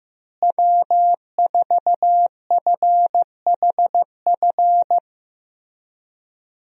Morse code, heard as a keyed tone, about 15 wpm, 700 Hz. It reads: W4FHF
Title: USA Call Signs - 5 Characters - Repeated 3x - 15wpm